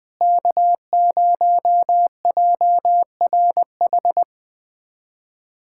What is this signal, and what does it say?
Morse code, heard as a keyed tone, about 20 wpm, 700 Hz. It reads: K0JRH
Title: USA Callsign DE Callsign 20wpm